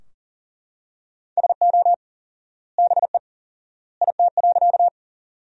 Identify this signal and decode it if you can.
Morse code, heard as a keyed tone, about 38 wpm, 700 Hz. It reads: SO BE IT.
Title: 4x Word-Spacing - Sentences from Top 100 Words - 40wpm